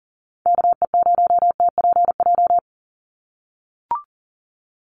Morse code, heard as a keyed tone, about 40 wpm, 700 Hz. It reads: KE0TPJ
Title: USA Call Signs 40wpm